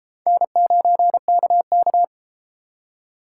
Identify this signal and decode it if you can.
Morse code, heard as a keyed tone, about 33 wpm, 700 Hz. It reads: N9KK